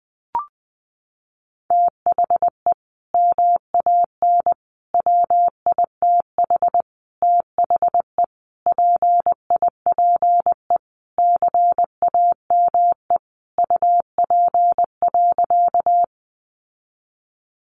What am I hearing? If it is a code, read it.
Morse code, heard as a keyed tone, about 20 wpm, 700 Hz. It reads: THE MAN WITH THE PIPE CAME UP.
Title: Sentences from Top 1500 Words 20wpm